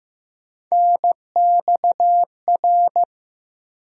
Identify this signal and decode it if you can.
Morse code, heard as a keyed tone, about 15 wpm, 700 Hz. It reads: NXR
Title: USA Call Sign Postfixes - 3 Letters 15wpm